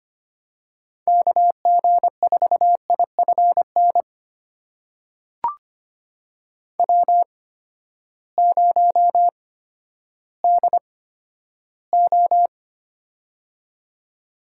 Morse code, heard as a keyed tone, about 25 wpm, spaced as for 12 wpm, 700 Hz. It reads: KG4IFN W0DO
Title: USA Call Signs - 8x Character Spacing - 25wpm